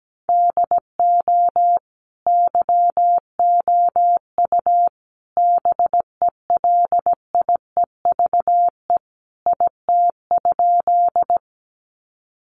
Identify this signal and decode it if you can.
Morse code, heard as a keyed tone, about 17 wpm, 700 Hz. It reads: DO YOU BELIEVE IT?
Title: Sentences from Top 1000 Words 17wpm